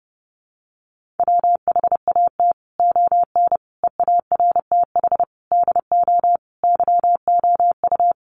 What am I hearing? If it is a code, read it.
Morse code, heard as a keyed tone, about 30 wpm, 700 Hz. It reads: WHAT ON EARTH DO YOU
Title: Sentences from Top 200 Words 30wpm